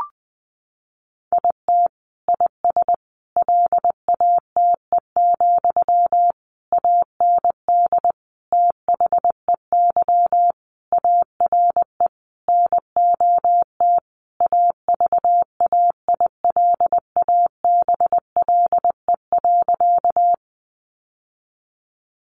Morse code, heard as a keyed tone, about 20 wpm, 700 Hz. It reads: IT IS LATE, AND THEY ARE NOT AVAILABLE.
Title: Sentences from Top 600 Words 20wpm